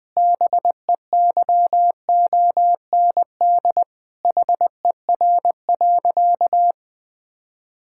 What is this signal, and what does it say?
Morse code, heard as a keyed tone, about 21 wpm, 700 Hz. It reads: BEYOND HER.
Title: Sentences from Top 1000 Words - Review Part 6 - 20wpm